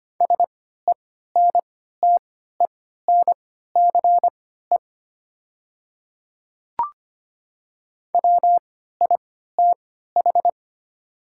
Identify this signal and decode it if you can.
Morse code, heard as a keyed tone, about 25 wpm, spaced as for 14 wpm, 700 Hz. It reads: SENTENCE WITH
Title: 3x Character Spacing - Top 500 Words - 25wpm